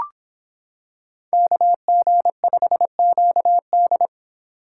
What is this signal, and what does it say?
Morse code, heard as a keyed tone, about 26 wpm, 700 Hz. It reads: KG5QD